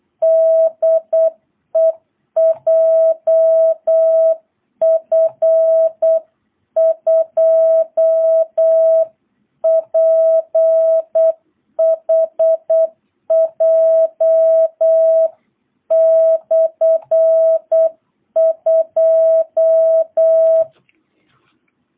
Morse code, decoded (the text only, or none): DE JF2PHJ/2